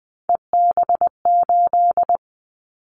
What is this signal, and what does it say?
Morse code, heard as a keyed tone, about 20 wpm, 700 Hz. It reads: EB8
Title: Callsign Prefixes for Top 50 DXCC - 20wpm